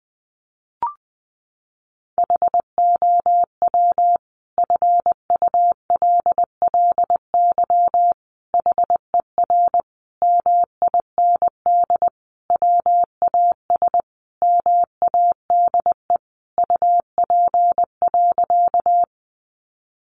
Morse code, heard as a keyed tone, about 20 wpm, 700 Hz. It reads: HOW FULLY HER MIND WAS MADE UP.
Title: Sentences from Top 1300 Words 20wpm